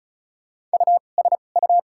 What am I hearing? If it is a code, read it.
Morse code, heard as a keyed tone, about 34 wpm, 700 Hz. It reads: U S U
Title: 2x Character Spacing - Top 500 Words - 35wpm